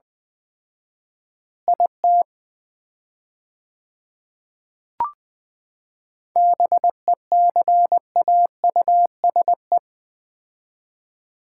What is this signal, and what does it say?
Morse code, heard as a keyed tone, about 20 wpm, 700 Hz. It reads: IT BECAUSE
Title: 4x Word-Spacing - Sets of 2 Words - 20wpm